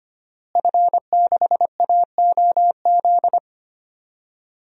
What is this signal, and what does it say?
Morse code, heard as a keyed tone, about 25 wpm, 700 Hz. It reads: F6AOZ
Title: DXCC-100 Call Signs - 5 Characters - Repeated 3x - 25wpm